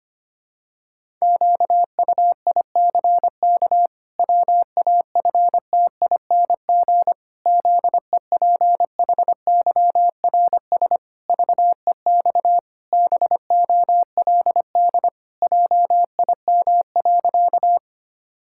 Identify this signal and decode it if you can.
Morse code, heard as a keyed tone, about 25 wpm, 700 Hz. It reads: QUICK WAFTING ZEPHYRS VEX BOLD JIM.
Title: Pangram Challenge v2 - 25wpm